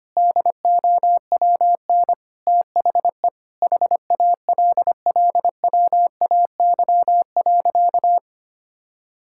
Morse code, heard as a keyed tone, about 25 wpm, 700 Hz. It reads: DOWN THE HALLWAY.
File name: practice / 25wpm / Sentences with 4-Word Phrases - 25wpm